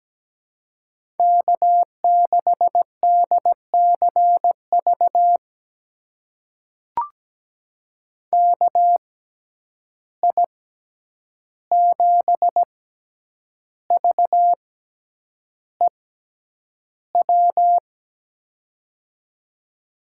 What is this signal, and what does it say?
Morse code, heard as a keyed tone, about 17 wpm, spaced as for 8 wpm, 700 Hz. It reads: K6DCV KI7VEW
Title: USA Call Signs - 6x Character Spacing - 17wpm